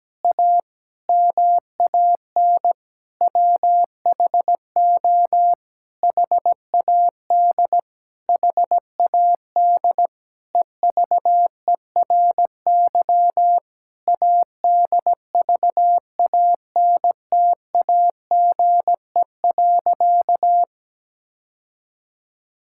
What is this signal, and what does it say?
Morse code, heard as a keyed tone, about 17 wpm, 700 Hz. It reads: A MAN WHO HAD HAD EVERY ADVANTAGE.
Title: Sentences from Top 1000 Words - Review Part 6 - 17wpm